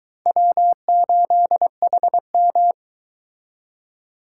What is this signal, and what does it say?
Morse code, heard as a keyed tone, about 23 wpm, 700 Hz. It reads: W8HM